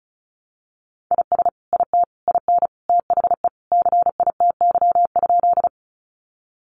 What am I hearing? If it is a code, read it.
Morse code, heard as a keyed tone, about 34 wpm, 700 Hz. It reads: IS IT IN THE CITY?
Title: Sentences from Top 300 Words 35wpm